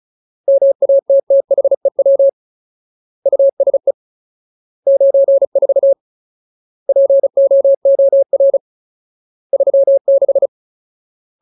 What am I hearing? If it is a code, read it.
Morse code, heard as a keyed tone, about 35 wpm, 550 Hz. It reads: MATTHEW USE 94 POOR 36
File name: Ultimate Word List / 35WPM Ultimate